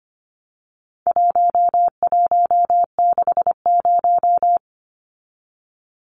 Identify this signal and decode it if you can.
Morse code, heard as a keyed tone, about 25 wpm, 700 Hz. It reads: 1160
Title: Numbers - 4-Digit 25wpm